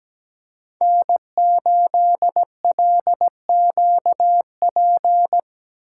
Morse code, heard as a keyed tone, about 17 wpm, 700 Hz. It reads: N8LQP
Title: USA Call Signs - 5 Characters - 3x Character Spacing - 17wpm